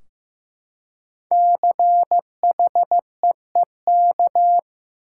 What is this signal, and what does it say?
Morse code, heard as a keyed tone, about 15 wpm, 700 Hz. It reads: CHEEK
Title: Five Letter Words - 15wpm